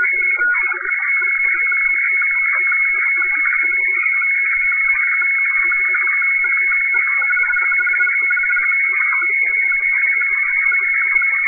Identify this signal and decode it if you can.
Weather fax (radiofax).
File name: hfwefax / 12788.1.khz. .2025